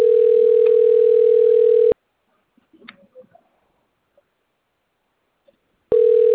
Ringback tone (North America), 440 + 480 Hz.
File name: rg-1600-unknown- - - .8301